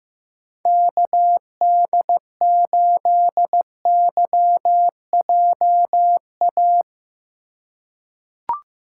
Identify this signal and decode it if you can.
Morse code, heard as a keyed tone, about 15 wpm, 700 Hz. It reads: KD8YJA
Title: USA Call Signs - 6 Characters - 2x Character Spacing - 15wpm